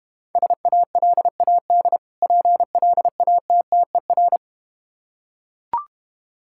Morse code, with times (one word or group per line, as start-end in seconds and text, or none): 0.4-2.0 SALAD
2.2-4.4 PLATTER